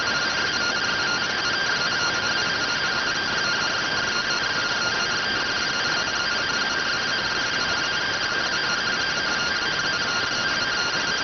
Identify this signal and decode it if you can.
Two-tone teleprinter signal, about 170 Hz shift.